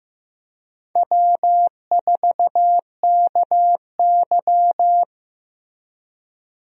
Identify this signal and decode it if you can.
Morse code, heard as a keyed tone, about 15 wpm, 700 Hz. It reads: W4KY
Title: USA Call Signs - 4 Characters - Repeated 2x - 15wpm